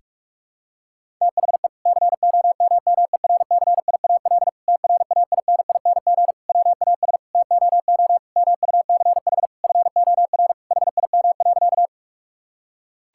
Morse code, heard as a keyed tone, about 42 wpm, 700 Hz. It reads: THE COMMERCIAL TRAINING WAS TOO MUCH FOR HIM.
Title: Sentences from Top 1000 Words - Review Part 5 - 45wpm